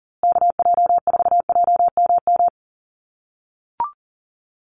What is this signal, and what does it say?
Morse code, heard as a keyed tone, about 39 wpm, 700 Hz. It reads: KJ4JMM
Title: USA Call Signs Repeated 3x 40wpm